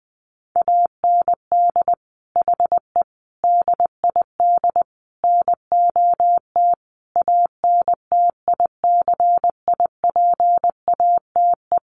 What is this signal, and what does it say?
Morse code, heard as a keyed tone, about 20 wpm, 700 Hz. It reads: AND HE DID NOT ANTICIPATE
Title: Sentences from Top 1500 Words 20wpm